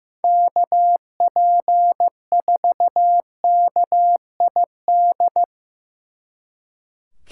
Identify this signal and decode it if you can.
Morse code, heard as a keyed tone, about 17 wpm, 700 Hz. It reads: KP4KID
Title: Speed Racing - USA 2x3 Call Signs 15wpm